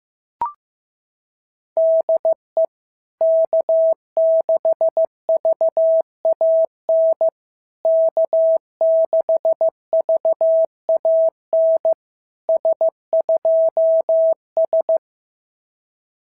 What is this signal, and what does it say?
Morse code, heard as a keyed tone, about 15 wpm, 650 Hz. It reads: DE K6VAN K6VAN S2S
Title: SOTA QSO Practice 15wpm